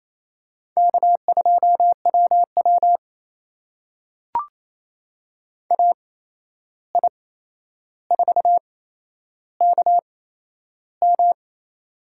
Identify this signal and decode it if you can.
Morse code, heard as a keyed tone, about 28 wpm, spaced as for 11 wpm, 700 Hz. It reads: K2WW AI4KM
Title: USA Call Signs - 8x Character Spacing - 28wpm